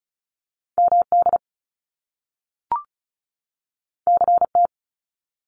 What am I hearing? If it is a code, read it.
Morse code, heard as a keyed tone, about 36 wpm, 700 Hz. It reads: MD CT
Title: US States 35wpm